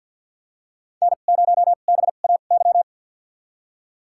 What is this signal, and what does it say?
Morse code, heard as a keyed tone, about 49 wpm, spaced as for 36 wpm, 700 Hz. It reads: N0BAY
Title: USA Call Signs - 5 Characters - 2x Character Spacing - 50wpm